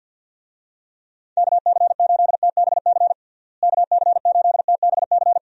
Morse code, heard as a keyed tone, about 51 wpm, 700 Hz. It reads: KC8TBC KC8TBC
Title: USA Call Signs - 6 Characters - Repeated 2x - 50wpm